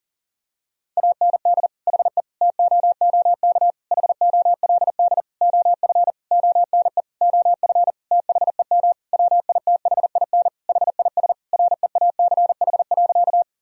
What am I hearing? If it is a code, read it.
Morse code, heard as a keyed tone, about 40 wpm, 700 Hz. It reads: AND HE TOOK HOLD OF ONE OF THEM WITHIN HIS REACH.
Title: Sentences from Top 1000 Words - Review Part 4 - 40wpm